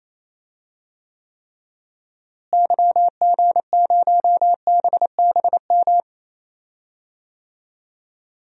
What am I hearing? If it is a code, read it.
Morse code, heard as a keyed tone, about 28 wpm, 700 Hz. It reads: YG0BBM
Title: DXCC-100 Call Signs - 6 Characters - Repeated 3x - 28wpm